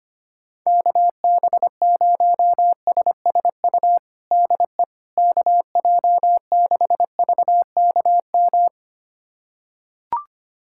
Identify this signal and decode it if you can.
Morse code, heard as a keyed tone, about 25 wpm, 700 Hz. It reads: KB0SSU DE KJ6VKM
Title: USA Callsign DE Callsign - 6 Characters - 25wpm